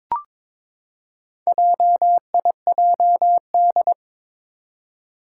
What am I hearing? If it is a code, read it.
Morse code, heard as a keyed tone, about 22 wpm, 700 Hz. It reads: JIJD
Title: Four Challenging Letter-Number Combinations 22wpm